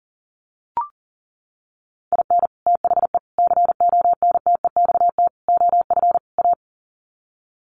Morse code, heard as a keyed tone, about 40 wpm, 700 Hz. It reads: IN THE CONTEXT OF A
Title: Sets of 5 Words Encore 40wpm